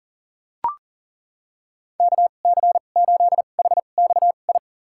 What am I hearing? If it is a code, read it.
Morse code, heard as a keyed tone, about 40 wpm, spaced as for 29 wpm, 700 Hz. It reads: KC8HXI
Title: USA Call Signs - 6 Characters - 2x Character Spacing - 40wpm